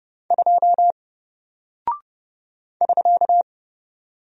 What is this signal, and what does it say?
Morse code, heard as a keyed tone, about 30 wpm, 700 Hz. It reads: 2 <SK>
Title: Single Character (Letter:Number:Punctuation:Prosign) Alt 30wpm